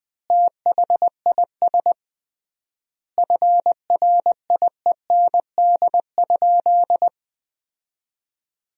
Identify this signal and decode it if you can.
Morse code, heard as a keyed tone, about 20 wpm, 700 Hz. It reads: THIS FRIEND?
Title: 3x Word-Spacing - Sentences from Top 300 Words - 20wpm